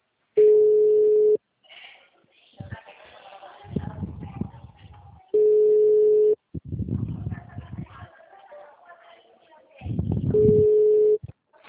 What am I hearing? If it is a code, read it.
Ringback tone (Europe, 425 Hz).